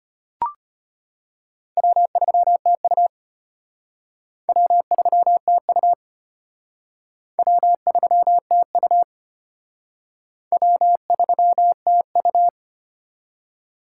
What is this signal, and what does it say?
Morse code, heard as a keyed tone, about 30 wpm, 700 Hz. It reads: W3TU W3TU W3TU W3TU
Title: Speed Racing - USA 1x2 Call Signs 25wpm